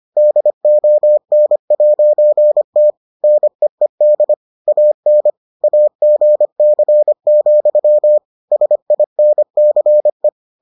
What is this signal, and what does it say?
Morse code, heard as a keyed tone, about 25 wpm, 600 Hz. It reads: DON'T NEED AN AGC, SINCE